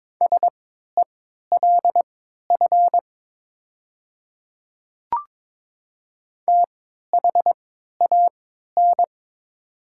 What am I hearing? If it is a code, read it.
Morse code, heard as a keyed tone, about 22 wpm, spaced as for 13 wpm, 700 Hz. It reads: SELF THAN
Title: 3x Character Spacing - Top 200 Words - 22wpm